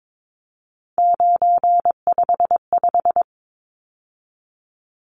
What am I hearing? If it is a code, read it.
Morse code, heard as a keyed tone, about 22 wpm, 700 Hz. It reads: 955
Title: Numbers - 3-Digit 22wpm